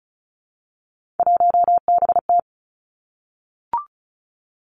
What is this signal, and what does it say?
Morse code, heard as a keyed tone, about 36 wpm, 700 Hz. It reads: 1BT
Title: Three Challenging Letter-Number Combinations 35wpm